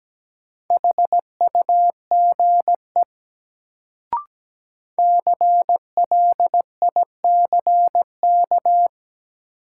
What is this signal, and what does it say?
Morse code, heard as a keyed tone, about 17 wpm, 700 Hz. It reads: HUGE CLICK
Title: Sentences from Top 1100 Words - New Word Drill - 17wpm